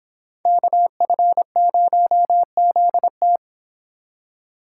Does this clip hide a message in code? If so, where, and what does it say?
Yes: Morse code from 0.5 to 3.4 s, heard as a keyed tone, about 26 wpm, 700 Hz. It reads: KF0ZT